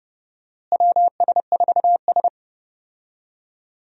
Morse code, heard as a keyed tone, about 30 wpm, 700 Hz. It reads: WS4S
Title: USA Call Signs - 4 Characters - Repeated 2x - 30wpm